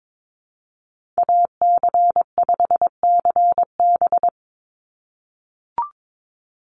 Morse code, heard as a keyed tone, about 22 wpm, 700 Hz. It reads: AC5CB
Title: USA Call Signs 22wpm